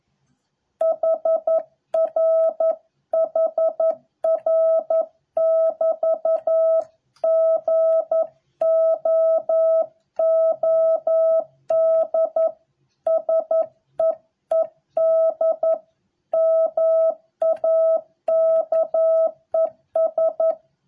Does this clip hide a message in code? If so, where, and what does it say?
Yes: Morse code from 0.8 to 20.5 s, heard as a keyed tone, about 11 wpm, 650 Hz. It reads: HR HR <BT> GOOD SEED MAKES